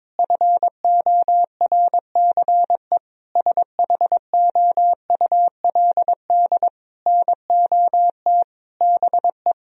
Morse code, heard as a keyed tone, about 22 wpm, 700 Hz. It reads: FORCE SHOULD NOT BE
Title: Sentences from Top 1000 Words - Review Part 4 - 22wpm